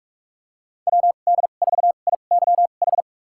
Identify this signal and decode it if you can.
Morse code, heard as a keyed tone, about 46 wpm, spaced as for 33 wpm, 700 Hz. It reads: WD4IYH
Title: USA Call Signs - 6 Characters - 2x Character Spacing - 45wpm